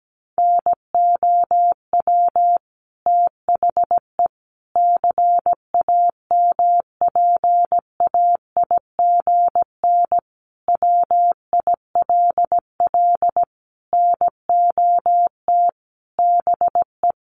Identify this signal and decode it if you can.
Morse code, heard as a keyed tone, about 17 wpm, 700 Hz. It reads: NOW THE CAMPAIGN WILL NOT BE
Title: Sentences from Top 1200 Words 17wpm